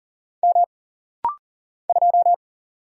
Morse code, heard as a keyed tone, about 40 wpm, 700 Hz. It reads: M 2
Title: Single Letter-Number Alt - Rapid-Fire 40wpm